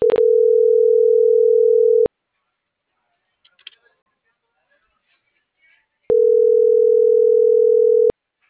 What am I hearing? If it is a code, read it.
Ringback tone (North America), 440 + 480 Hz.